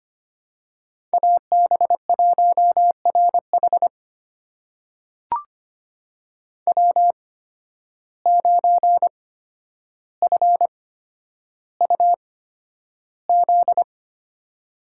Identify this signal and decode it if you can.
Morse code, heard as a keyed tone, about 25 wpm, spaced as for 11 wpm, 700 Hz. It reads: AB1RH W9FUZ
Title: USA Call Signs - 5 Characters - 8x Character Spacing - 25wpm